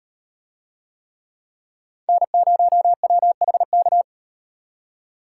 Morse code, heard as a keyed tone, about 38 wpm, 700 Hz. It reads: N0WHK